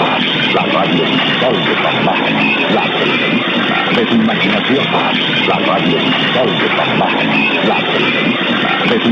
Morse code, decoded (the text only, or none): none